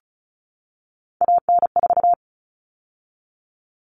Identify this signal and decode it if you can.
Morse code, heard as a keyed tone, about 34 wpm, 700 Hz. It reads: AN4
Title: All Cut Numbers - 3-Digit 35wpm